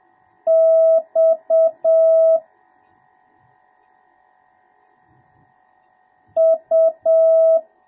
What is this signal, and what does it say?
Morse code, heard as a keyed tone, about 7 wpm, 1300 Hz, with noise. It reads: X U